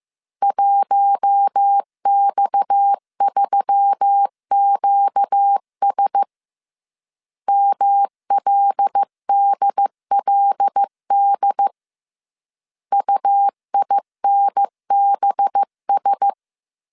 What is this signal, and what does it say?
Morse code, heard as a keyed tone, about 15 wpm, 800 Hz. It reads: 1X3QS MLDLD UINBS